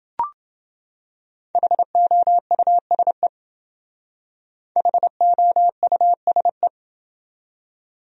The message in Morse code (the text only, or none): HOUSE HOUSE